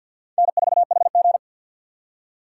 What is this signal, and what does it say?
Morse code, heard as a keyed tone, about 48 wpm, 700 Hz. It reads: N4HG
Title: USA Call Signs - 4 Characters - 6x Character Spacing - 50wpm